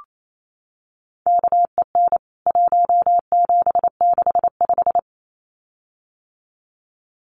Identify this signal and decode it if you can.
Morse code, heard as a keyed tone, about 28 wpm, 700 Hz. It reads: KEN 1765
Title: CWT Exchange 28wpm